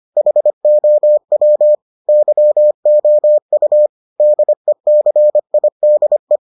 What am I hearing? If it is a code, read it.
Morse code, heard as a keyed tone, about 25 wpm, 600 Hz. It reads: HOW YOU DECIDE